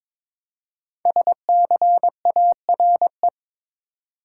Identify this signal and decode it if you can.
Morse code, heard as a keyed tone, about 22 wpm, 700 Hz. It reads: SCARE
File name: 5 Letter Words - First 3 Letters Given - 22wpm